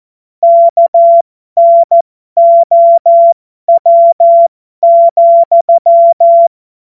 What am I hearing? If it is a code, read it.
Morse code, heard as a keyed tone, about 14 wpm, spaced as for 12 wpm, 680 Hz. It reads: KNOW,
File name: Morsekurs / Witze